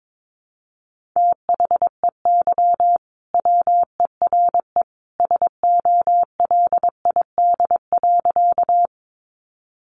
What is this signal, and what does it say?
Morse code, heard as a keyed tone, about 22 wpm, 700 Hz. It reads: THEY WERE SOLID.
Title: Sentences from Top 1500 Words 22wpm